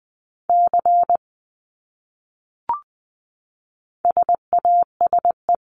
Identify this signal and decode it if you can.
Morse code, heard as a keyed tone, about 20 wpm, 700 Hz. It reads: C SASE
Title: Abbreviations Repeated 3x - 20wpm